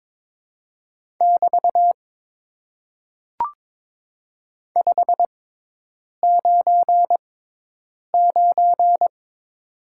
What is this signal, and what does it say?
Morse code, heard as a keyed tone, about 22 wpm, spaced as for 8 wpm, 700 Hz. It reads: <BT> 599
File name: Top 90 QSO Elements - 6x Character Spacing - 22wpm